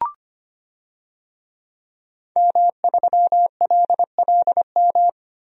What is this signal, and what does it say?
Morse code, heard as a keyed tone, about 25 wpm, 700 Hz. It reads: M3LLM
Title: DXCC-100 Call Signs - 5 Characters - Repeated 2x - 25wpm